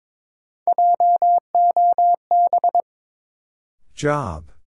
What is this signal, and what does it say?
Morse code, heard as a keyed tone, about 21 wpm, 700 Hz. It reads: JOB
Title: Sentences from Top 600 Words - New Word Drill - 22wpm